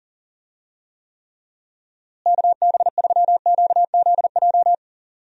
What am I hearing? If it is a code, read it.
Morse code, heard as a keyed tone, about 40 wpm, 700 Hz. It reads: KB3QZJ
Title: USA Call Signs - 6 Characters - Repeated 3x - 40wpm